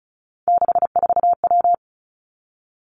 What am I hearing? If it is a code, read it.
Morse code, heard as a keyed tone, about 36 wpm, 700 Hz. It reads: 64W
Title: Three Challenging Letter-Number Combinations 35wpm